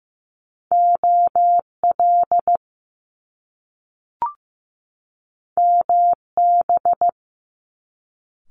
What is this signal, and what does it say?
Morse code, heard as a keyed tone, about 15 wpm, 700 Hz. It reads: OL MB
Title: Top Two Letter Combinations in Words 15wpm